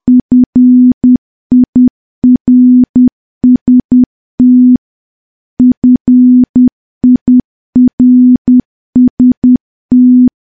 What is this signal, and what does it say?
Morse code, heard as a keyed tone, about 10 wpm, 250 Hz. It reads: FIRST FIRST
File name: morse / morse-training-10WPM